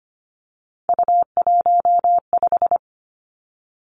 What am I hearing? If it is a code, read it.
Morse code, heard as a keyed tone, about 25 wpm, 700 Hz. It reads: U15
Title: Three Challenging Letter-Number Combinations 25wpm